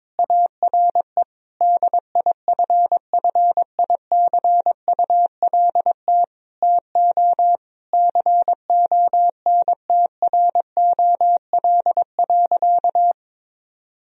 Morse code, heard as a keyed tone, about 22 wpm, 700 Hz. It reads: ARE DIFFICULT TO CONTROL.
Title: Sentences from Top 1000 Words - Review Part 4 - 22wpm